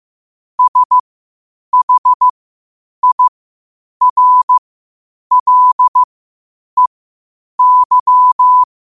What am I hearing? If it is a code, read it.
Morse code, heard as a keyed tone, about 15 wpm, spaced as for 8 wpm, 1000 Hz. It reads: SHIRLEY
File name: qotdfiles / 8wpm